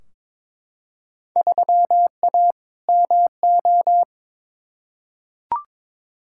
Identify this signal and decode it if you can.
Morse code, heard as a keyed tone, about 22 wpm, 700 Hz. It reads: 3A MO
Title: Field Day Exchanges Encore - 22wpm